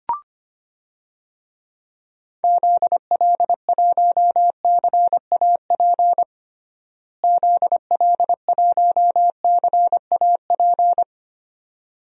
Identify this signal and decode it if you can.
Morse code, heard as a keyed tone, about 25 wpm, 700 Hz. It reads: ZL1CAP ZL1CAP
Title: DXCC-100 Call Signs - 6 Characters - Repeated 2x - 25wpm